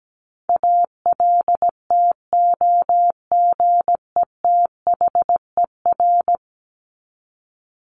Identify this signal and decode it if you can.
Morse code, heard as a keyed tone, about 17 wpm, 700 Hz. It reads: ALTOGETHER
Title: Top 1501-1750 Words 17wpm